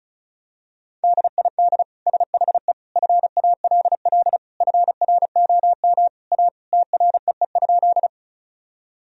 Morse code, heard as a keyed tone, about 34 wpm, 700 Hz. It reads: DID SHE FALL FROM A TREE?
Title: Sentences from Top 1000 Words - Review Part 2 - 35wpm